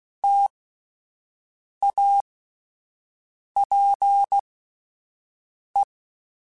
Morse code, heard as a keyed tone, about 16 wpm, spaced as for 5 wpm, 800 Hz. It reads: TAPE
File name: Alphabetical / Alphabetical mp3s / Ts @5